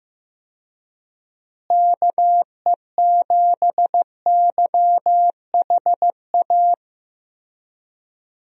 Morse code, heard as a keyed tone, about 15 wpm, 700 Hz. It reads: KE7YHA
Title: USA Call Signs - 6 Characters - Repeated 3x - 15wpm